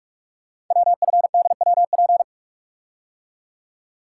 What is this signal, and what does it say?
Morse code, heard as a keyed tone, about 44 wpm, 700 Hz. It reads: WFDWP
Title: Code Groups - 5 Letters - 45wpm